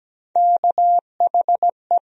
Morse code, heard as a keyed tone, about 17 wpm, 700 Hz. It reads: KHE
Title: USA Call Sign Postfixes - 3 Letters 17wpm